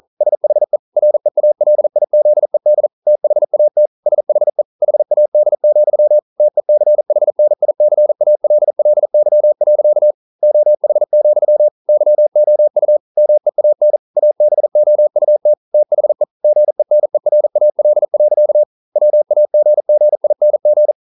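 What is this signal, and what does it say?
Morse code, heard as a keyed tone, about 42 wpm, 600 Hz. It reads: SHE REALIZED THAT SHE HAD, TECHNICALLY. OH, YOU MEAN ABOUT THE GENERAL. WAGGING
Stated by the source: Morse code practice at 41 wpm, from CW med Gnister.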